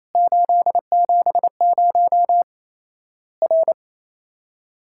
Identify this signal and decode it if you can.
Morse code, heard as a keyed tone, about 28 wpm, 700 Hz. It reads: 870 R
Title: Worked All Europe DX Contest - QTC Exchange - 28wpm